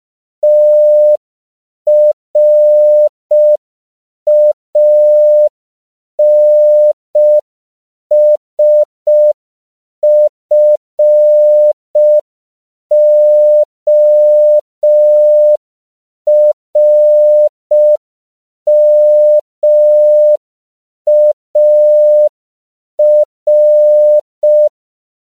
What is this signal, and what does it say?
Morse code, heard as a keyed tone, about 5 wpm, 600 Hz. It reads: TRANSFORMAR
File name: 5PPM